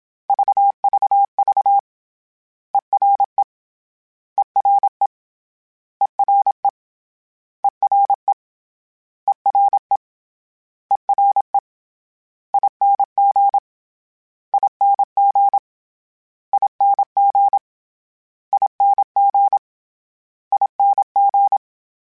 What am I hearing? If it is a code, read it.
Morse code, heard as a keyed tone, about 27 wpm, 800 Hz. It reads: VVV ERE ERE ERE ERE ERE ERE ING ING ING ING ING
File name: Special INGs-mp3s / Special INGs-mp3s / Ere ing @25